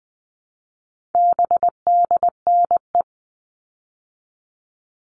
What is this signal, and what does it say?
Morse code, heard as a keyed tone, about 20 wpm, 700 Hz. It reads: BDNE
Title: All Cut Numbers - 4-Digit 20wpm